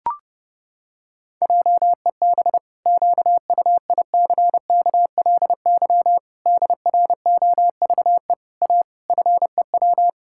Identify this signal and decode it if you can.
Morse code, heard as a keyed tone, about 30 wpm, 700 Hz. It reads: JEB QUICKLY DROVE A FEW
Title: Pangram Challenge v2 - 30wpm